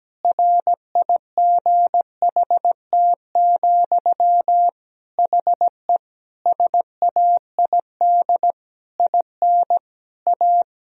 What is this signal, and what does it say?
Morse code, heard as a keyed tone, about 17 wpm, 700 Hz. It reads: RIGHT, HE SAID IN A
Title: Sentences from Top 1000 Words - Review Part 1 - 17wpm